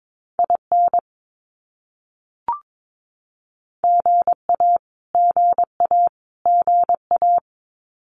Morse code, heard as a keyed tone, about 22 wpm, 700 Hz. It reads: IN GA GA GA
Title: US States Repeated 3x 22wpm